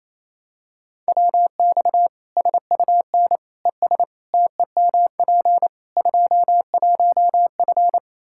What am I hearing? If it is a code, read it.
Morse code, heard as a keyed tone, about 28 wpm, 700 Hz. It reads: WX SUN ES TEMP 21F
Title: Practice QSO - Exchange 2 - Info - 28wpm